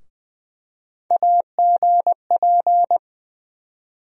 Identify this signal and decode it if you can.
Morse code, heard as a keyed tone, about 20 wpm, 700 Hz. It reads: AGP
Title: USA Call Sign Postfixes - 3 Letters 20wpm